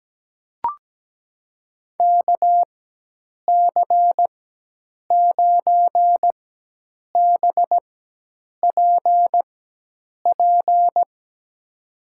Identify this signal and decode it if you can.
Morse code, heard as a keyed tone, about 17 wpm, spaced as for 8 wpm, 700 Hz. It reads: KC9BPP
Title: USA Call Signs - 6 Characters - 4x Character Spacing - 17wpm